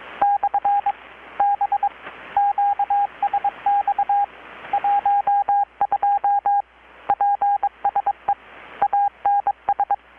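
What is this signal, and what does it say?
Morse code, heard as a keyed tone, about 22 wpm, 810 Hz, with radio noise. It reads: /B QSX 12 PSE ANS
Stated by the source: Morse from KPH's Night of Nights XX maritime radiotelegraphy event, received in Chiapas, Mexico about 3,800 km away on a Kenwood R-600 with a grounded 10 m long-wire antenna.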